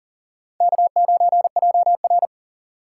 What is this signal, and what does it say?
Morse code, heard as a keyed tone, about 40 wpm, 700 Hz. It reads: K9JR
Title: USA Call Signs - 4 Characters - 4x Character Spacing - 40wpm